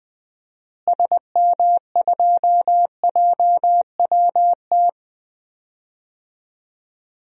Morse code, heard as a keyed tone, about 21 wpm, 700 Hz. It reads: SM2JWT
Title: DXCC-100 Call Signs - 6 Characters - Repeated 2x - 20wpm